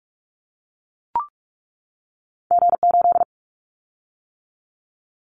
Morse code, heard as a keyed tone, about 44 wpm, 700 Hz. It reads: G8
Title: Callsign Prefixes for Top 25 DXCC - 45wpm